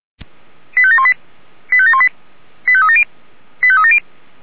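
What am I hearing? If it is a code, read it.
Five-tone selective call (ZVEI), heard as quick runs of five tones, tones of about 70 ms. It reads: 85718 85718 85370 85370
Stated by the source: ZVEI five-tone call received by SDR.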